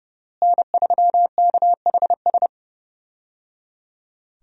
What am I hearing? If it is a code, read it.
Morse code, heard as a keyed tone, about 30 wpm, 700 Hz. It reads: N3KHS
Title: USA Call Signs - 5 Characters - Repeated 3x - 30wpm